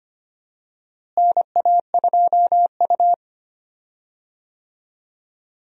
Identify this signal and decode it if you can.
Morse code, heard as a keyed tone, about 25 wpm, 700 Hz. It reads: NA2U
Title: USA Call Signs - 4 Characters - Repeated 2x - 25wpm